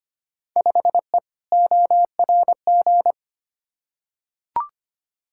Morse code, heard as a keyed tone, about 25 wpm, 700 Hz. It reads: 5E ORG
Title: Field Day Exchanges Encore - 25wpm